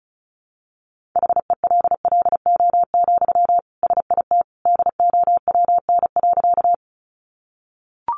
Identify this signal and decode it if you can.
Morse code, heard as a keyed tone, about 34 wpm, 700 Hz. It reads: HELLO, SIT DOWN.
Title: Sentences from Top 1100 Words 35wpm